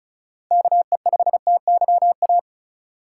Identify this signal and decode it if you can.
Morse code, heard as a keyed tone, about 36 wpm, 700 Hz. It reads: KE5TYA
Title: USA Call Signs - 6 Characters - 4x Character Spacing - 35wpm